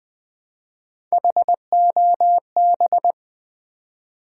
Morse code, heard as a keyed tone, about 20 wpm, 700 Hz. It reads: HOB
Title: Build A Word - 1 Letter At A Time - 5-Letter Words - 20wpm